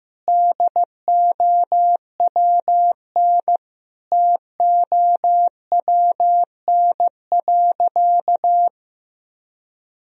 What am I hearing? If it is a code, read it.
Morse code, heard as a keyed tone, about 15 wpm, 700 Hz. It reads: DOWNTOWN.
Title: Sentences from Top 1000 Words - Review Part 4 - 15wpm